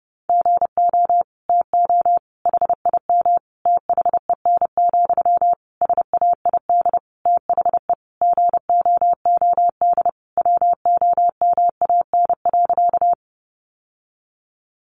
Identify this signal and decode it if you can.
Morse code, heard as a keyed tone, about 30 wpm, 700 Hz. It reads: GO TO HIM THEN, SAID THE GOOD WOMAN.
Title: Sentences from Top 1100 Words 30wpm